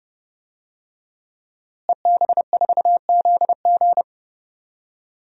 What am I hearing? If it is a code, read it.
Morse code, heard as a keyed tone, about 32 wpm, 700 Hz. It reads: EB4ZG
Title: DXCC-100 Call Signs - 5 Characters - Repeated 3x - 30wpm